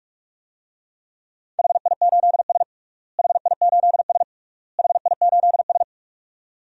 Morse code, heard as a keyed tone, about 44 wpm, 700 Hz. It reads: HI8S HI8S HI8S
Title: DXCC-100 Call Signs - 4 Characters - Repeated 3x - 45wpm